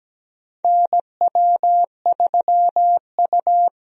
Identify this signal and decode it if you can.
Morse code, heard as a keyed tone, about 17 wpm, 700 Hz. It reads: NW3U
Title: USA Call Signs - 4 Characters - 17wpm